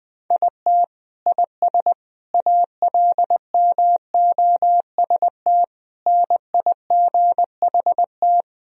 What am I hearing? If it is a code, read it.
Morse code, heard as a keyed tone, about 20 wpm, 700 Hz. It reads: IT IS ALMOST NIGHT
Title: Sentences from Top 1000 Words - Review Part 4 - 20wpm